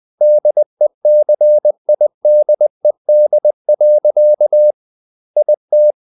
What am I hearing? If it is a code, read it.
Morse code, heard as a keyed tone, about 20 wpm, 600 Hz. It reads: DECIDED. IT
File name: the old man and the sea part 2 20wpm